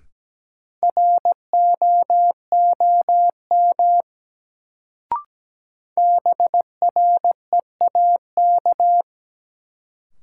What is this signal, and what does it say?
Morse code, heard as a keyed tone, about 17 wpm, 700 Hz. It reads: ROOM BREAK
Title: Sets of 4 Words - New Word Drill - 17wpm